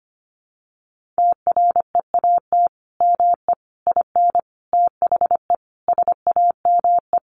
Morse code, heard as a keyed tone, about 25 wpm, 700 Hz. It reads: TREAT ME IN THE SAME
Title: Sentences from Top 1300 Words 25wpm